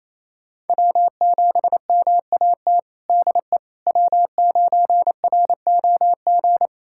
Morse code, heard as a keyed tone, about 28 wpm, 700 Hz. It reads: W7MAT DE W9ROG
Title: USA Callsign DE Callsign - 5 Characters - 28wpm